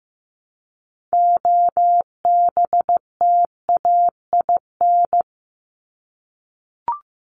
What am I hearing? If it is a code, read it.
Morse code, heard as a keyed tone, about 15 wpm, 700 Hz. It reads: OBTAIN
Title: ED Suffix - Top 100 Words with Base Words 15wpm